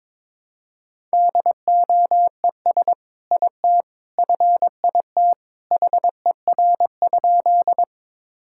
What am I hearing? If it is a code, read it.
Morse code, heard as a keyed tone, about 22 wpm, 700 Hz. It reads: DOES IT FIT HER?
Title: Sentences from Top 1000 Words - Review Part 5 - 22wpm